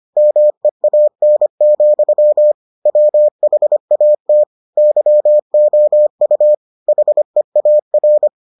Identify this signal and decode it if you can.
Morse code, heard as a keyed tone, about 25 wpm, 600 Hz. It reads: MEAN, WHAT YOU HEAR